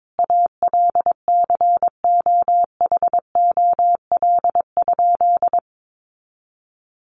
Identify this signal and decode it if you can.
Morse code, heard as a keyed tone, about 22 wpm, 700 Hz. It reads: ALCOHOL?
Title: Sentences from Top 1400 Words 22wpm